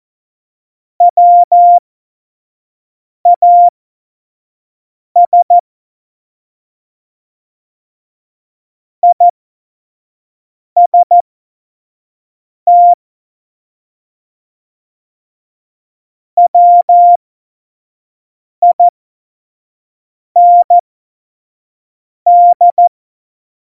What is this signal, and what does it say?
Morse code, heard as a keyed tone, about 14 wpm, spaced as for 5 wpm, 700 Hz. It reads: WAS IST WIND
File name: Morsekurs / Witze / Witz 14. Mai 2019, 25 F70